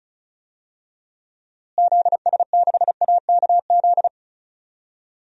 Morse code, heard as a keyed tone, about 34 wpm, 700 Hz. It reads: ZS6AKZ
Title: DXCC-100 Call Signs - 6 Characters - Repeated 3x - 35wpm